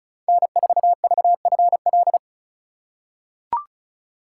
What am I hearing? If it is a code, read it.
Morse code, heard as a keyed tone, about 36 wpm, 700 Hz. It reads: N4VFL
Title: USA Call Signs - 5 Characters - 3x Character Spacing - 35wpm